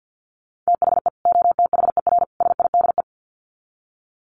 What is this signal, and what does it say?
Morse code, heard as a keyed tone, about 50 wpm, 700 Hz. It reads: THE OTHER SIDE
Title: Sets of 3 Words 50wpm